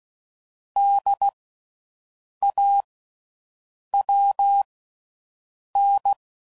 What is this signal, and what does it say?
Morse code, heard as a keyed tone, about 16 wpm, spaced as for 6 wpm, 800 Hz. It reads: DAWN
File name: Suffixes / Special INGs / Slow With Faster Ings